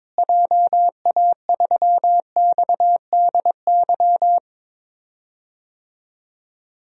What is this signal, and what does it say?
Morse code, heard as a keyed tone, about 21 wpm, 700 Hz. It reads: JA3XDY
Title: DXCC-100 Call Signs - 6 Characters - Repeated 2x - 22wpm